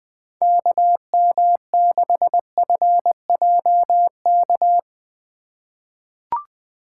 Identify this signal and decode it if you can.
Morse code, heard as a keyed tone, about 20 wpm, 700 Hz. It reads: KM6FJK
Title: USA Call Signs - 6 Characters - 2x Character Spacing - 20wpm